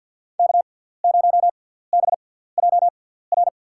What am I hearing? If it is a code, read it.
Morse code, heard as a keyed tone, about 52 wpm, spaced as for 17 wpm, 700 Hz. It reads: K0BJR
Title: USA Call Signs - 6x Character Spacing - 50wpm